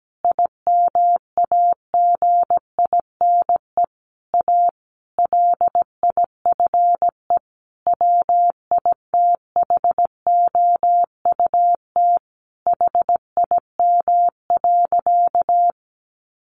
Morse code, heard as a keyed tone, about 17 wpm, 700 Hz. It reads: IMAGINE A LIFE WITHOUT HIM.
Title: Sentences from Top 1400 Words 17wpm